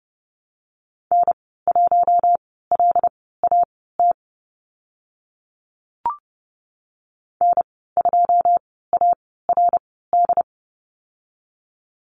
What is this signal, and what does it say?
Morse code, heard as a keyed tone, about 30 wpm, spaced as for 17 wpm, 700 Hz. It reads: N1LAT N2ARD
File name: Lesson 14g - Number 2 - Call Signs 30wpm